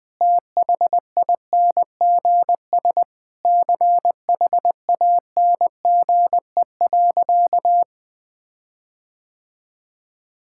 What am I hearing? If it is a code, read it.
Morse code, heard as a keyed tone, about 19 wpm, 700 Hz. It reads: THINGS CHANGE.
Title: Sentences with 6-Word Phrases - Spoken Beginning - 20wpm